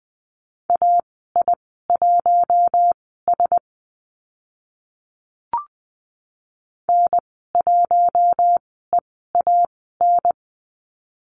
Morse code, heard as a keyed tone, about 20 wpm, spaced as for 14 wpm, 700 Hz. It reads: AI1S N1EAN
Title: 042 Lesson 8g - Number 1 - Call Signs 20wpm